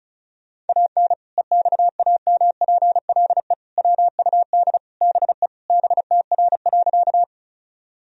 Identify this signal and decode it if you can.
Morse code, heard as a keyed tone, about 34 wpm, 700 Hz. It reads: AN EXAMPLE WUD BE BTR.
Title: Sentences from Top 400 Words and Abbreviations 35wpm